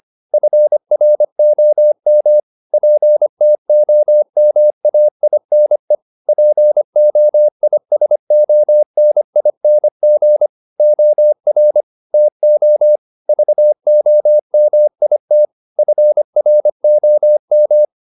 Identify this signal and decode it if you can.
Morse code, heard as a keyed tone, about 25 wpm, 600 Hz. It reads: FROM PTOMAINE POISONING OR TO VOMIT FROM
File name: the old man and the sea part 1 25wpm